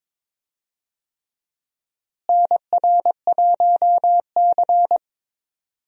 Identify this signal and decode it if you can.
Morse code, heard as a keyed tone, about 22 wpm, 700 Hz. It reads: NR1C
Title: USA Call Signs - 4 Characters - Repeated 3x - 22wpm